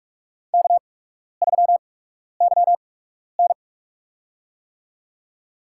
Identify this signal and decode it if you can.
Morse code, heard as a keyed tone, about 43 wpm, spaced as for 12 wpm, 690 Hz. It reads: K3YN
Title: USA Call Signs - 4 Characters - 8x Character Spacing - 45wpm